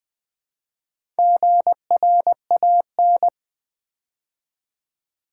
Morse code, heard as a keyed tone, about 20 wpm, 700 Hz. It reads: GRAN
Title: Build A Word - 4 Letters At A Time - 11-Letter Words - 20wpm